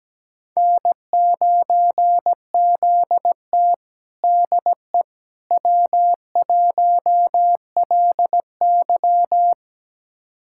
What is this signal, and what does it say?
Morse code, heard as a keyed tone, about 17 wpm, 700 Hz. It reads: N9ZT DE W1LY
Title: USA Callsign DE Callsign - 4 Characters - 17wpm